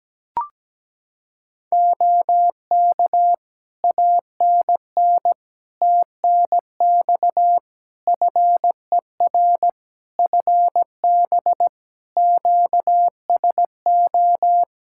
Morse code, heard as a keyed tone, about 17 wpm, 700 Hz. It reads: OK ANN TNX FER FB QSO
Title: Practice QSO - Exchange 4 - Ending - 17wpm